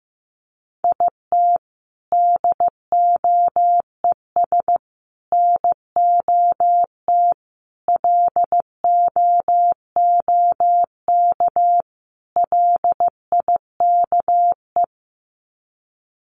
Morse code, heard as a keyed tone, about 15 wpm, 700 Hz. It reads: IT DOES NOT LOOK LIKE
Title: Sets of 5 Words Encore2 15wpm